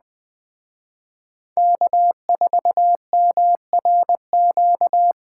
Morse code, heard as a keyed tone, about 21 wpm, 700 Hz. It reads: K4MRQ